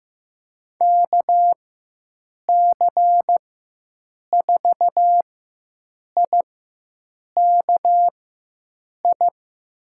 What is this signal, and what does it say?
Morse code, heard as a keyed tone, about 15 wpm, spaced as for 7 wpm, 700 Hz. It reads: KC4IKI
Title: USA Call Signs - 6 Characters - 4x Character Spacing - 15wpm